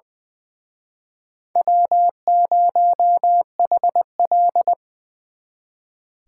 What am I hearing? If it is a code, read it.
Morse code, heard as a keyed tone, about 23 wpm, 700 Hz. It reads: W0HL
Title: Speed Racing - USA 1x2 Call Signs 20wpm